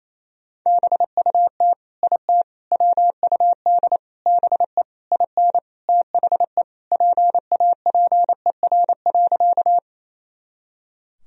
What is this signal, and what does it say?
Morse code, heard as a keyed tone, about 28 wpm, 700 Hz. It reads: BUT IT WUD BE IN THE PAPER.
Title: Sentences from Top 300 Words and Abbreviations 28wpm